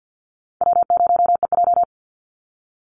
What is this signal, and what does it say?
Morse code, heard as a keyed tone, about 50 wpm, 700 Hz. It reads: W0EJ
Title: Easy Call Signs - 50wpm